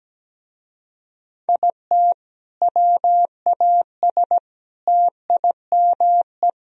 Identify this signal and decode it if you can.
Morse code, heard as a keyed tone, about 18 wpm, 700 Hz. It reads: IT WAS TIME
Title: Speed-Racing - Sets of 4 Words 17wpm